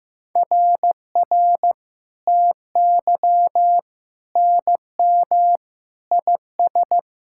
Morse code, heard as a keyed tone, about 15 wpm, 700 Hz. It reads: RR TY NM IS
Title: SOTA QSO Practice 15wpm